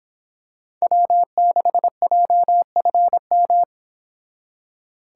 Morse code, heard as a keyed tone, about 26 wpm, 700 Hz. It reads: W6JFM